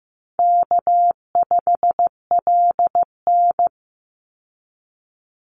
Morse code, heard as a keyed tone, about 15 wpm, 700 Hz. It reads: K5LN
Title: Easy Call Signs Repeated 3x - 15wpm